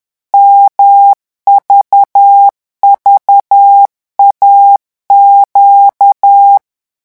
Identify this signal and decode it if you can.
Morse code, heard as a keyed tone, about 10 wpm, 800 Hz. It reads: MVVAQ
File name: morse mp3 / revision2-vitesse 10 mots